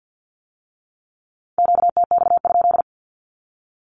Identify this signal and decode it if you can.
Morse code, heard as a keyed tone, about 50 wpm, 700 Hz. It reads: QTX?
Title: Q-Codes with Spoken Meaning - 50wpm